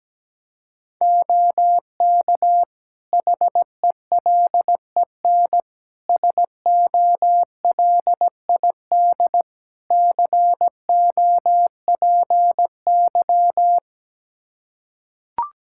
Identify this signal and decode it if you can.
Morse code, heard as a keyed tone, about 17 wpm, 700 Hz. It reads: OK HELEN SOLID COPY
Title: Practice QSO - Exchange 3 - More Info - 17wpm